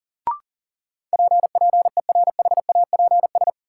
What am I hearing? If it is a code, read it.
Morse code, heard as a keyed tone, about 40 wpm, 700 Hz. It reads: P PERHAPS
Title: Sentences from Top 1100 Words - New Word Drill - 40wpm